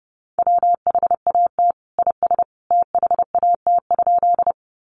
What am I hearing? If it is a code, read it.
Morse code, heard as a keyed tone, about 28 wpm, 700 Hz. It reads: WHAT IS THAT?
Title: Sentences from Top 100 Words 30wpm